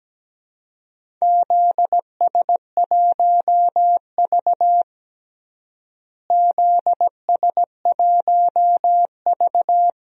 Morse code, heard as a keyed tone, about 17 wpm, 700 Hz. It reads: ZS1V ZS1V
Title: DXCC-100 Call Signs - 4 Characters - Repeated 2x - 17wpm